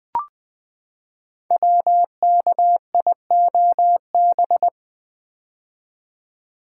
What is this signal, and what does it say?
Morse code, heard as a keyed tone, about 20 wpm, 700 Hz. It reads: WKIOB
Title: Code Groups - 5 Letters - 20wpm